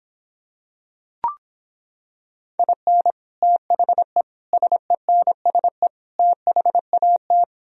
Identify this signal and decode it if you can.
Morse code, heard as a keyed tone, about 26 wpm, 700 Hz. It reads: IN THE SENSE THAT